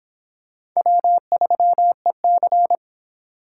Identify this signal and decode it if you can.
Morse code, heard as a keyed tone, about 26 wpm, 700 Hz. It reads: W3EC